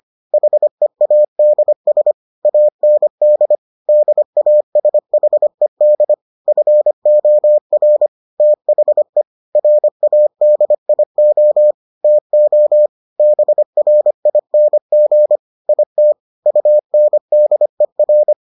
Morse code, heard as a keyed tone, about 25 wpm, 600 Hz. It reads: HEADS AND DASHED FOR THE RADIO TO BRING IT UNDER